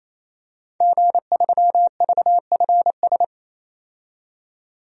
Morse code, heard as a keyed tone, about 28 wpm, 700 Hz. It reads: G3VFS
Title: DXCC-100 Call Signs - 5 Characters - 28wpm